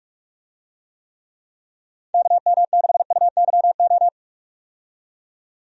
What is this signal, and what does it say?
Morse code, heard as a keyed tone, about 46 wpm, 700 Hz. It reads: KM6UYO
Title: USA Call Signs - 6 Characters - 45wpm